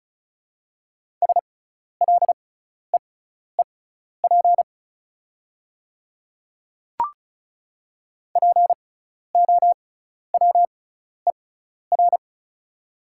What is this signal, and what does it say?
Morse code, heard as a keyed tone, about 34 wpm, spaced as for 12 wpm, 700 Hz. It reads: SLEEP POWER
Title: 6x Character Spacing - Top 500 Words - 35wpm